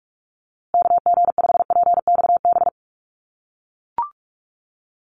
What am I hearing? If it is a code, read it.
Morse code, heard as a keyed tone, about 44 wpm, 700 Hz. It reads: KG5PXB
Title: USA Call Signs Repeated 2x 45wpm